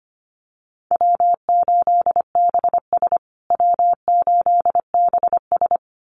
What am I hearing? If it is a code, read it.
Morse code, heard as a keyed tone, about 25 wpm, 700 Hz. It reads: W8BS W8BS
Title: USA Call Signs Repeated 2x 25wpm